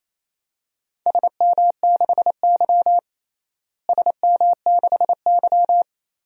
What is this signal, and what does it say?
Morse code, heard as a keyed tone, about 27 wpm, 700 Hz. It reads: SM6Y SM6Y
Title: DXCC-100 Call Signs - 4 Characters - Repeated 3x - 28wpm